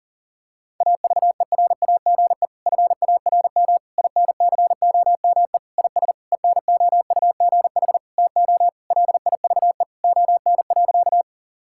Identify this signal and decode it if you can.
Morse code, heard as a keyed tone, about 40 wpm, 700 Hz. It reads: AVERAGE FARM INCOME IS ENOUGH TO LIVE ON.
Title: Sentences from Top 1000 Words - Review Part 6 - 40wpm